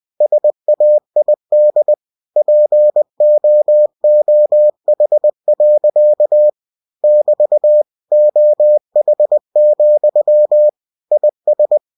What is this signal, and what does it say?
Morse code, heard as a keyed tone, about 20 wpm, 600 Hz. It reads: SAID POOH. <BT> OH, IS
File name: cw-books / winnie-the-pooh / winnie-20wpm-page-0050